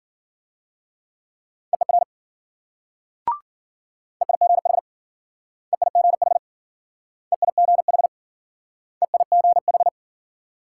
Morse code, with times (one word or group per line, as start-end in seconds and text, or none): 1.7-2.0 EER
4.2-4.8 EIGH
5.7-6.4 EIGH
7.3-8.1 EIGH
9.0-9.9 EIGH